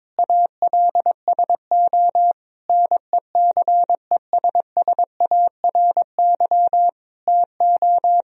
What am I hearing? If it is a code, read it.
Morse code, heard as a keyed tone, about 22 wpm, 700 Hz. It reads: ALSO NECESSARY TO
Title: Sentences from Top 1000 Words - Review Part 2 - 22wpm